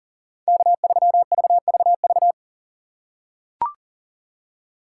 Morse code, heard as a keyed tone, about 40 wpm, 700 Hz. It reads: K3VVV
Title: USA Call Signs - 5 Characters - 2x Character Spacing - 40wpm